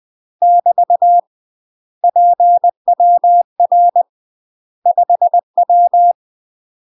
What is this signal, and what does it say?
Morse code, heard as a keyed tone, about 20 wpm, 700 Hz. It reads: <BT> PWR 5W
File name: UK MASTER QSOs TRAINING / 20WPM / UK MASTER QSOs 0064